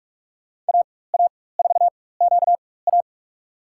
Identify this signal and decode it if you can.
Morse code, heard as a keyed tone, about 46 wpm, spaced as for 21 wpm, 700 Hz. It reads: AA4QA
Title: USA Call Signs - 5 Characters - 4x Character Spacing - 45wpm